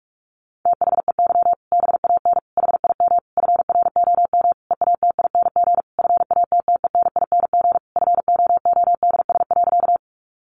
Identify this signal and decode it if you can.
Morse code, heard as a keyed tone, about 44 wpm, 700 Hz. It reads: THEY BAN HIM FROM EATING FATTENING FOODS.
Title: Easy Phrases 45wpm